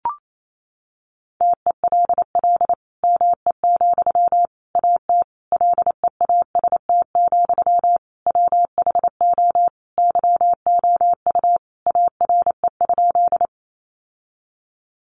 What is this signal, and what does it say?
Morse code, heard as a keyed tone, about 28 wpm, 700 Hz. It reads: TELL ME, AT LEAST, WHO YOU ARE?
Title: Sentences from Top 700 Words 28wpm